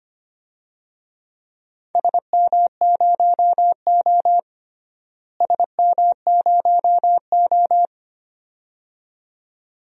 Morse code, heard as a keyed tone, about 25 wpm, 700 Hz. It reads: SM0O SM0O
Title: DXCC-100 Call Signs - 4 Characters - Repeated 2x - 25wpm